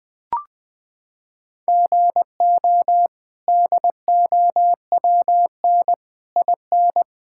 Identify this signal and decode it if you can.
Morse code, heard as a keyed tone, about 20 wpm, 700 Hz. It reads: GO DOWN IN
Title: Sets of 8 Words - 20wpm